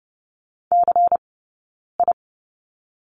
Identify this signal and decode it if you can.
Morse code, heard as a keyed tone, about 30 wpm, spaced as for 9 wpm, 690 Hz. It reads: CI
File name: Lesson 19d - Letter w - Two Characters 30wpm